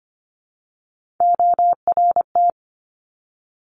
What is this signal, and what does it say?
Morse code, heard as a keyed tone, about 25 wpm, 700 Hz. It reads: ORT
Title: Top Three Letter Combinations in Words 25wpm